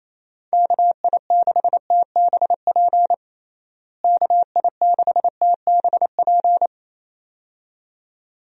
Morse code, heard as a keyed tone, about 28 wpm, 700 Hz. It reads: KI6TBP KI6TBP
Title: USA Call Signs - 6 Characters - Repeated 2x - 28wpm